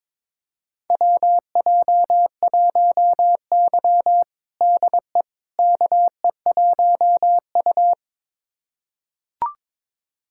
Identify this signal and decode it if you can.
Morse code, heard as a keyed tone, about 22 wpm, 700 Hz. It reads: WJ1Y DE KE1U
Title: USA Callsign DE Callsign - 4 Characters - 22wpm